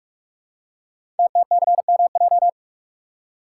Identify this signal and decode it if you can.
Morse code, heard as a keyed tone, about 44 wpm, 690 Hz. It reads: TTCMJ
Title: Code Groups - 5 Letters - 45wpm